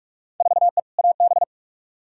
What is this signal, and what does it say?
Morse code, heard as a keyed tone, about 41 wpm, 700 Hz. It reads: 4E AB
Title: Field Day Exchanges Encore - 45wpm